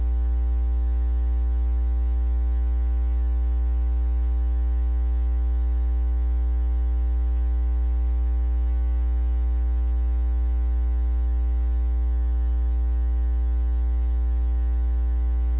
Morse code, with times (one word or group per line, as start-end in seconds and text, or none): none